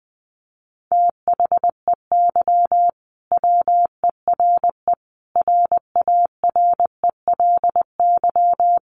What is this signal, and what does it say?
Morse code, heard as a keyed tone, about 20 wpm, 700 Hz. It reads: THEY WERE RARELY
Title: Sentences from Top 1400 Words 20wpm